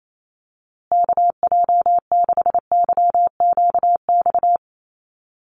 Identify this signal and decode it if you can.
Morse code, heard as a keyed tone, about 28 wpm, 700 Hz. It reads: KJ6YQX
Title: USA Call Signs Repeated 3x 28wpm